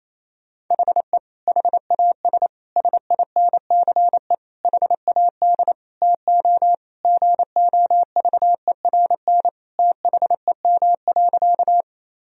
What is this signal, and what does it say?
Morse code, heard as a keyed tone, about 28 wpm, 700 Hz. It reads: HE HAS SINCE HAD TO GOVERN THEM.
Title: Sentences from Top 1000 Words - Review Part 2 - 28wpm